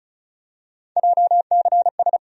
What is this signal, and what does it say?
Morse code, heard as a keyed tone, about 36 wpm, 700 Hz. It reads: JCS
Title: USA Call Sign Postfixes - 3 Letters 35wpm